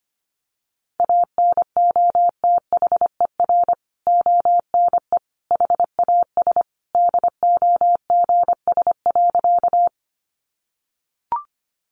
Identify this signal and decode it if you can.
Morse code, heard as a keyed tone, about 25 wpm, 700 Hz. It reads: ANOTHER ONE HAS DOGS.
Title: Additional Easy Phrases 25wpm